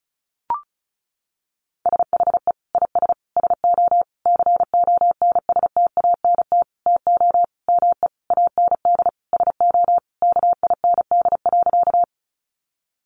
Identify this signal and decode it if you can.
Morse code, heard as a keyed tone, about 34 wpm, 700 Hz. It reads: SHE IS SO CONSTANT TO ME AND SO KIND.
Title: Sentences from Top 1500 Words 35wpm